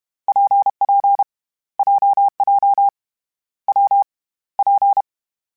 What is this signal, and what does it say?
Morse code, heard as a keyed tone, about 32 wpm, 800 Hz. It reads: PP JJ W P
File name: Difficult Combos / Difficult Combos-mp3s / WPJ WPJ1 @30